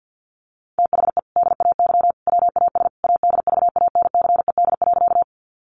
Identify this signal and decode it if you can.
Morse code, heard as a keyed tone, about 50 wpm, 700 Hz. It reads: THE DAY WAS ADVANCED.
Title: Sentences from Top 1200 Words 50wpm